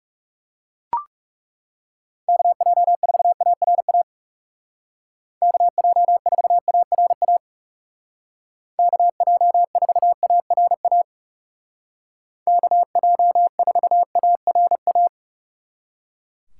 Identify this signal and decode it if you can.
Morse code, heard as a keyed tone, about 40 wpm, 700 Hz. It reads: KJ4ARA KJ4ARA KJ4ARA KJ4ARA